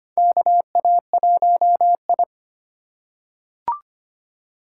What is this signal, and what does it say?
Morse code, heard as a keyed tone, about 25 wpm, 700 Hz. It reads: KA1I
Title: USA Call Signs - 4 Characters - 2x Character Spacing - 25wpm